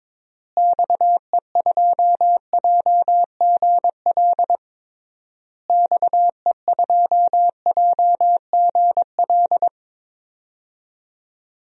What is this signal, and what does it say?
Morse code, heard as a keyed tone, about 22 wpm, 700 Hz. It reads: XE2JGL XE2JGL
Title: DXCC-100 Call Signs - 6 Characters - Repeated 3x - 22wpm